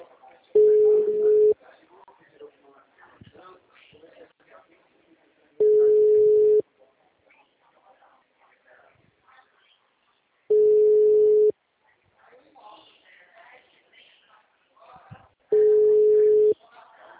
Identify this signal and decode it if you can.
Ringback tone (Europe, 425 Hz), with speech.